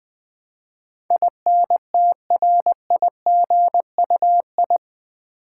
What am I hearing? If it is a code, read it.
Morse code, heard as a keyed tone, about 20 wpm, 700 Hz. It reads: INTRIGUI
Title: Build A Word - 4 Letters At A Time - 10-Letter Words - 20wpm